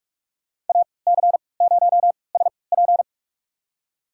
Morse code, heard as a keyed tone, about 46 wpm, spaced as for 25 wpm, 700 Hz. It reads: AC0SP
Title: USA Call Signs - 5 Characters - 3x Character Spacing - 45wpm